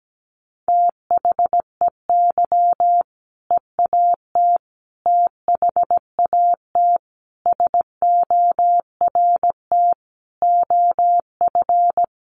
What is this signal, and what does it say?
Morse code, heard as a keyed tone, about 17 wpm, 700 Hz. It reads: THEY EAT THAT SORT OF
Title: Sentences from Top 1400 Words 17wpm